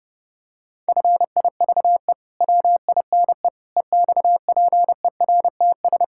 Morse code, heard as a keyed tone, about 32 wpm, 700 Hz. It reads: FIVE WINE EXPERTS
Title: Pangram Challenge v2 - 30wpm